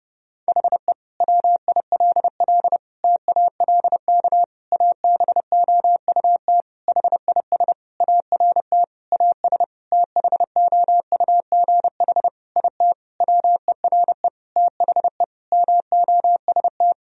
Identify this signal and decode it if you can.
Morse code, heard as a keyed tone, about 30 wpm, 700 Hz. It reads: HE WILL TALK ABOUT HIS ART AS THOUGH IT WERE THE MOST
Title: Sentences from Top 1000 Words - Review Part 3 - 30wpm